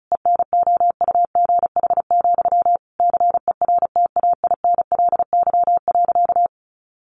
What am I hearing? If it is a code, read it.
Morse code, heard as a keyed tone, about 34 wpm, 700 Hz. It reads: ENOUGH, CERTAINLY.
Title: Sentences from Top 900 Words 35wpm